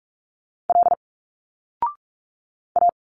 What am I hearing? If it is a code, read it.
Morse code, heard as a keyed tone, about 46 wpm, 700 Hz. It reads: L A
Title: Single Letters 45wpm